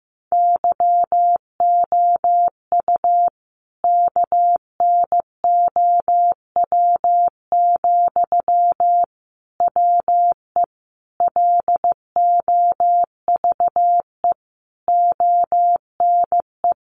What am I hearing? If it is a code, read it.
Morse code, heard as a keyed tone, about 15 wpm, 700 Hz. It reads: YOU KNOW, WE LOVE ONE
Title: Sentences from Top 600 Words 15wpm